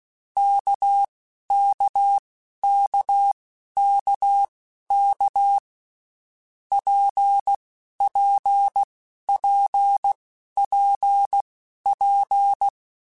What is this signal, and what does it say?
Morse code, heard as a keyed tone, about 16 wpm, spaced as for 11 wpm, 800 Hz. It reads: KKKKK PPPPP
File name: Basic Letter Groups By7Groups / RKPX / RKPX@10